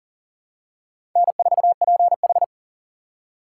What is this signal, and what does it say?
Morse code, heard as a keyed tone, about 40 wpm, 700 Hz. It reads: N4PH
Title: USA Call Signs - 4 Characters - 6x Character Spacing - 40wpm